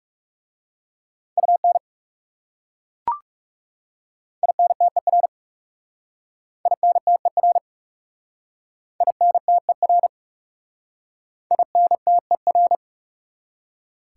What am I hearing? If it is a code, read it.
Morse code, heard as a keyed tone, about 37 wpm, 700 Hz. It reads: UN INTER INTER INTER INTER
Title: Speed-Racing Prefixes and Suffixes 30wpm